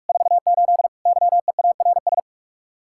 Morse code, heard as a keyed tone, about 49 wpm, 700 Hz. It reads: 48 YEARS